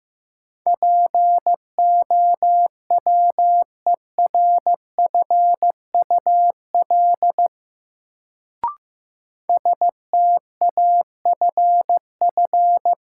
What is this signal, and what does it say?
Morse code, heard as a keyed tone, about 15 wpm, 700 Hz. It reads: POWERFUL STAFF
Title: Sets of 5 Words Encore - New Word Drill - 15wpm